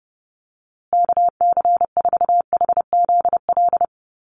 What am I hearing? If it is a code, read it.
Morse code, heard as a keyed tone, about 30 wpm, 700 Hz. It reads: KC4HZL
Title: USA Call Signs Repeated 3x 30wpm